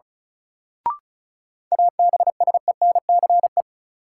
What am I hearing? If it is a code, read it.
Morse code, heard as a keyed tone, about 36 wpm, 700 Hz. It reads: ABSENCE
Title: Sets of 4 Words - New Word Drill - 35wpm